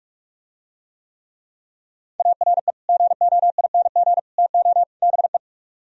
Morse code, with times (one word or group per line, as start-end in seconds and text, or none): 2.2-2.7 ARE
2.9-4.2 GOING
4.4-4.8 TO
5.0-5.4 BE